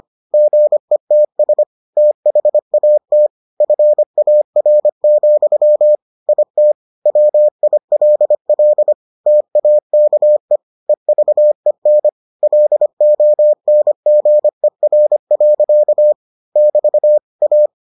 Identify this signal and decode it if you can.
Morse code, heard as a keyed tone, about 25 wpm, 600 Hz. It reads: GETS THAT FAR, IT WILL TAKE EVEN LONGER. <BT> A